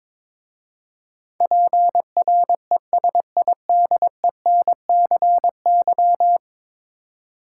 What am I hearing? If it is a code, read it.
Morse code, heard as a keyed tone, about 22 wpm, 700 Hz. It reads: PRESIDENCY
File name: practice / 22wpm / 10 Letter Words - First 3 Letters Given - 22wpm